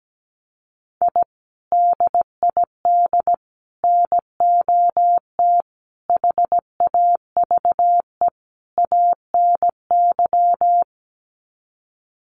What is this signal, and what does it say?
Morse code, heard as a keyed tone, about 17 wpm, 700 Hz. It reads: I DID NOT HAVE ANY
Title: Sets of 5 Words Encore 17wpm